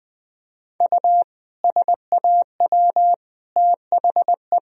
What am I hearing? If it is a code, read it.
Morse code, heard as a keyed tone, about 20 wpm, 700 Hz. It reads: U SAW THE
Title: Sentences from Top 400 Words and Abbreviations 20wpm